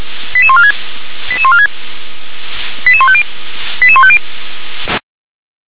Five-tone selective call (ZVEI), heard as quick runs of five tones, tones of about 70 ms. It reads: 88146 88146 88150 88150